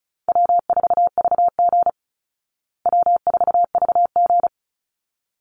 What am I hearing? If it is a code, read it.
Morse code, heard as a keyed tone, about 34 wpm, 700 Hz. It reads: W4VG W4VG
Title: Call Signs Repeated 3x - 35wpm